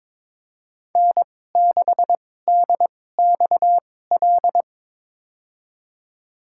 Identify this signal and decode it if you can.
Morse code, heard as a keyed tone, about 22 wpm, spaced as for 16 wpm, 700 Hz. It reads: N6DXL
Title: USA Call Signs - 5 Characters - 2x Character Spacing - 22wpm